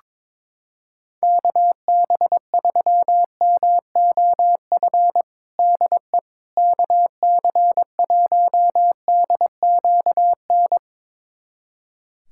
Morse code, heard as a keyed tone, about 22 wpm, 700 Hz. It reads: KB3MOF DE KC1DQN
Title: USA Callsign DE Callsign 22wpm